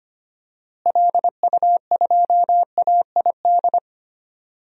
Morse code, heard as a keyed tone, about 25 wpm, 700 Hz. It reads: LU2AID
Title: DXCC-100 Call Signs - 6 Characters - Repeated 3x - 25wpm